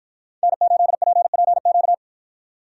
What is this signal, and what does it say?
Morse code, heard as a keyed tone, about 53 wpm, 700 Hz. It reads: N8PPQ